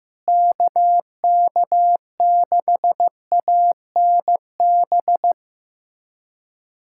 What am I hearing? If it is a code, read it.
Morse code, heard as a keyed tone, about 15 wpm, 700 Hz. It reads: KK6ANB
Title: USA Callsign DE Callsign 15wpm